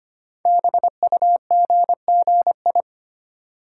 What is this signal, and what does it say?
Morse code, heard as a keyed tone, about 25 wpm, 700 Hz. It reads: BUGGI
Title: Code Groups - 5 Letter:Numbers - 25wpm